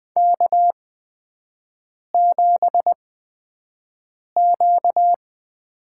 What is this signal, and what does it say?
Morse code, heard as a keyed tone, about 20 wpm, spaced as for 5 wpm, 700 Hz. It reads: K7Q
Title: USA Call Signs - 4 Characters - 8x Character Spacing - 20wpm